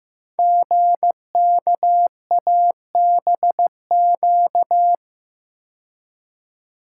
Morse code, heard as a keyed tone, about 15 wpm, 700 Hz. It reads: GKABQ
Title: Code Groups - 5 Letters - 15wpm